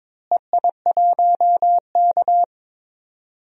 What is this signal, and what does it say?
Morse code, heard as a keyed tone, about 22 wpm, 700 Hz. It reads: EI1K
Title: DXCC-100 Call Signs - 4 Characters - Repeated 2x - 22wpm